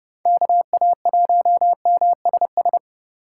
Morse code, heard as a keyed tone, about 30 wpm, 700 Hz. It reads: KA1MSS
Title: USA Call Signs - 6 Characters - 2x Character Spacing - 30wpm